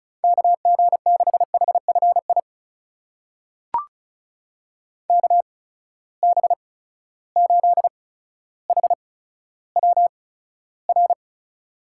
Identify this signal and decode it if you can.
Morse code, heard as a keyed tone, about 36 wpm, spaced as for 15 wpm, 700 Hz. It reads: KG6HFI KB8HWR
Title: USA Call Signs - 6 Characters - 8x Character Spacing - 35wpm